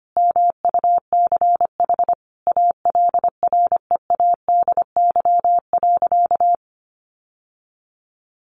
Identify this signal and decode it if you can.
Morse code, heard as a keyed tone, about 25 wpm, 700 Hz. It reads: MUCH ALREADY.
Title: Sentences from Top 700 Words 25wpm